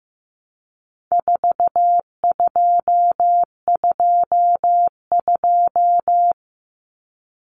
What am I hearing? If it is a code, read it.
Morse code, heard as a keyed tone, about 15 wpm, 700 Hz. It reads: 4222
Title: Common Cut Numbers - 4-Digit 15wpm